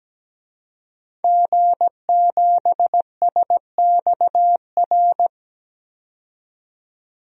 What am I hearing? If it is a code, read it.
Morse code, heard as a keyed tone, about 17 wpm, 700 Hz. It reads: G7SXR
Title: DXCC-100 Call Signs - 5 Characters - 17wpm